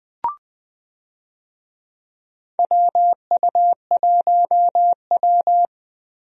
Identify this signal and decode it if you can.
Morse code, heard as a keyed tone, about 20 wpm, 700 Hz. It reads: WU1W
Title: USA Call Signs - 4 Characters - 20wpm